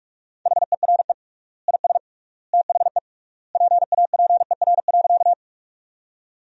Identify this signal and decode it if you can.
Morse code, heard as a keyed tone, about 46 wpm, 700 Hz. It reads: HERE IS THE PAPER.
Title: 3x Word-Spacing - Sentences from Top 300 Words - 45wpm